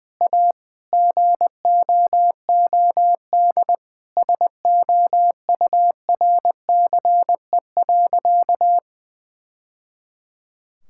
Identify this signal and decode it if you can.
Morse code, heard as a keyed tone, about 21 wpm, 700 Hz. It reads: A GOOD SOURCE.
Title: Sentences from Top 1000 Words - Review Part 5 - 20wpm